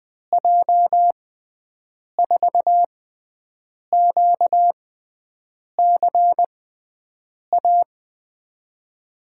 Morse code, heard as a keyed tone, about 20 wpm, spaced as for 7 wpm, 700 Hz. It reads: J4QCA
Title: USA Call Signs - 6 Characters - 6x Character Spacing - 20wpm